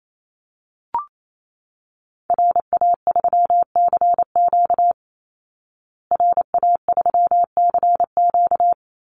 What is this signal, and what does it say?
Morse code, heard as a keyed tone, about 28 wpm, 700 Hz. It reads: RA3CQ RA3CQ
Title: Call Signs Repeated 3x - 28wpm